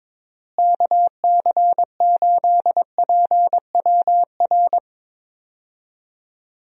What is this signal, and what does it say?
Morse code, heard as a keyed tone, about 22 wpm, 700 Hz. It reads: KC8PWR
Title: USA Call Signs - 6 Characters - Repeated 3x - 22wpm